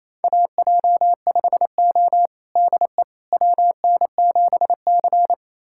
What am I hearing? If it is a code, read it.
Morse code, heard as a keyed tone, about 28 wpm, 700 Hz. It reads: AJ5O DE WN7C
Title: USA Callsign DE Callsign - 4 Characters - 28wpm